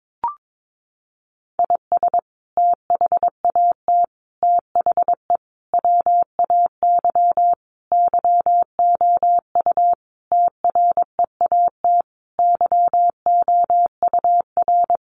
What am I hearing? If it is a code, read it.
Morse code, heard as a keyed tone, about 22 wpm, 700 Hz. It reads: IS THAT THE WAY YOU TREAT YOUR
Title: Sentences from Top 1300 Words 22wpm